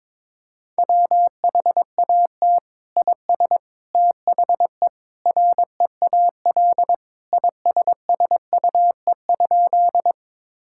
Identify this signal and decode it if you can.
Morse code, heard as a keyed tone, about 22 wpm, 700 Hz. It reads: WHAT IS THE REAL ISSUE?
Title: Sentences from Top 1000 Words - Review Part 6 - 22wpm